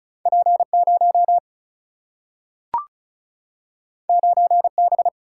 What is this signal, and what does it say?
Morse code, heard as a keyed tone, about 36 wpm, 700 Hz. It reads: P0 9B
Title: Single-Letter Single-Number 35wpm